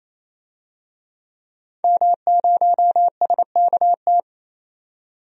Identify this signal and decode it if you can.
Morse code, heard as a keyed tone, about 28 wpm, 700 Hz. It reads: M0SKT
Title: DXCC-100 Call Signs - 5 Characters - 28wpm